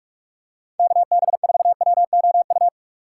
Morse code, heard as a keyed tone, about 46 wpm, 700 Hz. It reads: KB4WOU
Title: USA Call Signs - 6 Characters - 4x Character Spacing - 45wpm